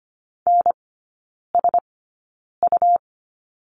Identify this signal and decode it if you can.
Morse code, heard as a keyed tone, about 25 wpm, spaced as for 9 wpm, 700 Hz. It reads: NSU
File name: Lesson 17e - Letter u - Three Characters 25wpm